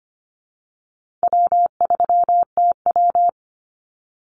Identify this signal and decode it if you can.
Morse code, heard as a keyed tone, about 25 wpm, 700 Hz. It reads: W3TW
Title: Easy Call Signs Repeated 3x - 25wpm